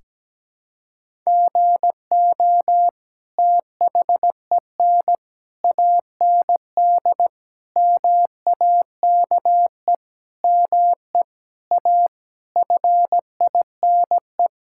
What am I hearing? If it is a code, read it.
Morse code, heard as a keyed tone, about 17 wpm, 700 Hz. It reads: GO THEN AND MAKE ME A FINE
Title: Sentences from Top 1000 Words - Review Part 5 - 17wpm